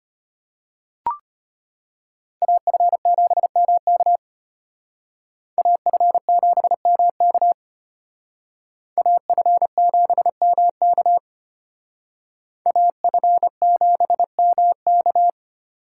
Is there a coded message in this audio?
Yes: Morse code from 2.4 to 15.3 s, heard as a keyed tone, about 33 wpm, 700 Hz. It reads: AF7MK AF7MK AF7MK AF7MK